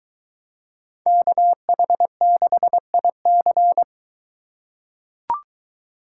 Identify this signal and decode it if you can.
Morse code, heard as a keyed tone, about 22 wpm, 700 Hz. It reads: KH6IC